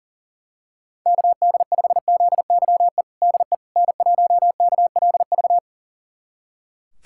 Morse code, heard as a keyed tone, about 40 wpm, 700 Hz. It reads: KD5ZYE DE N1KLV
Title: USA Callsign DE Callsign 40wpm